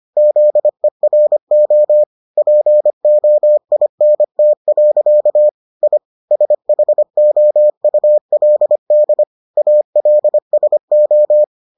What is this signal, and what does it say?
Morse code, heard as a keyed tone, about 25 wpm, 600 Hz. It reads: ZERO POINT. I SHOULD ALSO